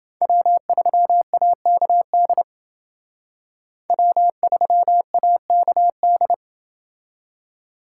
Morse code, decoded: W3AKD W3AKD

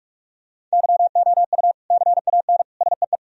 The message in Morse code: YOU CAN SEE

